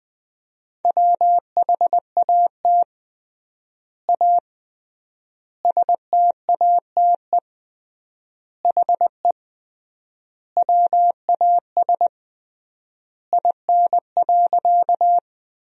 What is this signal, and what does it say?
Morse code, heard as a keyed tone, about 20 wpm, 700 Hz. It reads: WHAT A STATE HE WAS IN.
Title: 3x Word-Spacing - Sentences from Top 400 Words - 20wpm